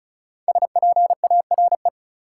Morse code, heard as a keyed tone, about 34 wpm, 700 Hz. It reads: SPARE
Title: Five Letter Words - 35wpm